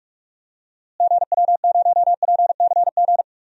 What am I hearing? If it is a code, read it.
Morse code, heard as a keyed tone, about 41 wpm, 700 Hz. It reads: GW0PCG
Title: DXCC-100 Call Signs - 6 Characters - Repeated 3x - 45wpm